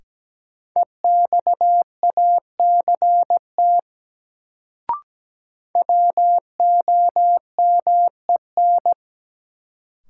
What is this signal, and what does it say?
Morse code, heard as a keyed tone, about 17 wpm, 700 Hz. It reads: EXACT WOMEN
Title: Sentences from Top 1300 Words - New Word Drill Alt - 17wpm